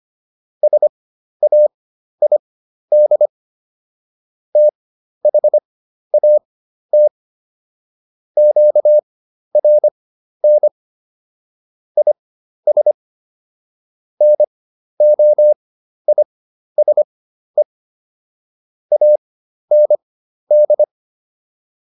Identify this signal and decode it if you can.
Morse code, heard as a keyed tone, about 25 wpm, spaced as for 12 wpm, 600 Hz. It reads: SAID THAT QRN IS NOISE AND